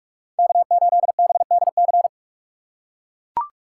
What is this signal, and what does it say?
Morse code, heard as a keyed tone, about 44 wpm, 700 Hz. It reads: K8BDC
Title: USA Call Signs - 5 Characters - 2x Character Spacing - 45wpm